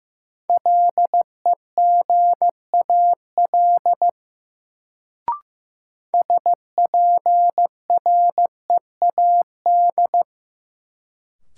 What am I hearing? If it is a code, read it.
Morse code, heard as a keyed tone, about 15 wpm, 700 Hz. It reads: LEGAL SPREAD
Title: Sentences from Top 1200 Words - New Word Drill - 15wpm